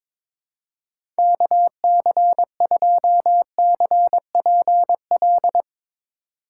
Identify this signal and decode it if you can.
Morse code, heard as a keyed tone, about 22 wpm, 700 Hz. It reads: KC2CPL
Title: USA Call Signs - 6 Characters - 2x Character Spacing - 22wpm